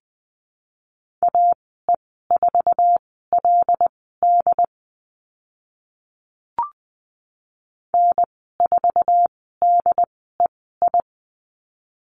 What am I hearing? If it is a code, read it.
Morse code, heard as a keyed tone, about 20 wpm, spaced as for 14 wpm, 700 Hz. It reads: AE4LD N4DEI
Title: 076 Lesson 13g - Letter L - Call Signs 20wpm